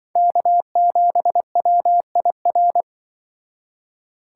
Morse code, heard as a keyed tone, about 25 wpm, 700 Hz. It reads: K7WIR
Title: Speed Racing - USA 1x3 Call Signs 20wpm